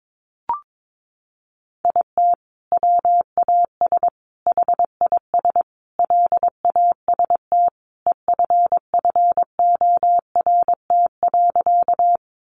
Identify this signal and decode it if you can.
Morse code, heard as a keyed tone, about 22 wpm, 700 Hz. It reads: IT WAS HIS LAST EFFORT.
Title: Sentences from Top 1200 Words 22wpm